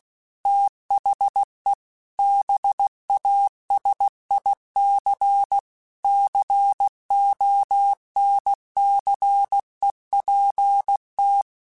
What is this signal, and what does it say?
Morse code, heard as a keyed tone, about 16 wpm, 800 Hz. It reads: THE BASIC CONCEPT
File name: Technical Subjects / Technical Subjects-mp3s / AND Gate @15